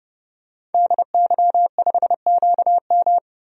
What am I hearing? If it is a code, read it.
Morse code, heard as a keyed tone, about 31 wpm, 700 Hz. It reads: DY5QM
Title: DXCC-100 Call Signs - 5 Characters - 30wpm